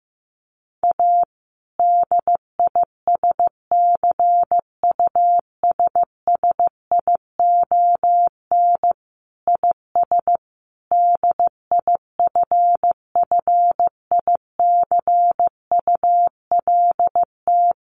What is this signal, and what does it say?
Morse code, heard as a keyed tone, about 15 wpm, 700 Hz. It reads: A DISCUSSION IS DIFFICULT
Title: Sentences from Top 1100 Words 15wpm